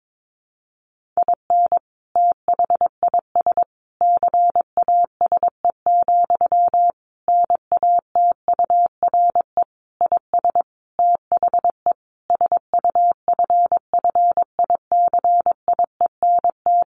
Morse code, heard as a keyed tone, about 22 wpm, 700 Hz. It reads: IN THIS CASE, NATURE IS THE SUFFICIENT
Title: Sentences from Top 1400 Words 22wpm